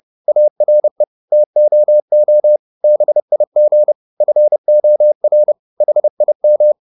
Morse code, heard as a keyed tone, about 30 wpm, 600 Hz. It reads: ARE TOO BIG FOR HIM
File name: the old man and the sea / the old man and the sea 1 30wpm